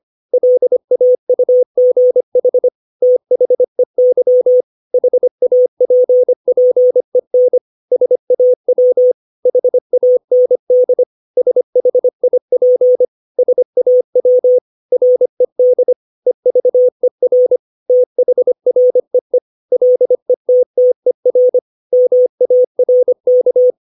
Morse code, heard as a keyed tone, about 25 wpm, 500 Hz. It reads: LAUGH THEY HAPPEN SAW HAND SHIP SAW RED EVER THREE LETTER MARK
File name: 632 words / 632 Words 25WPM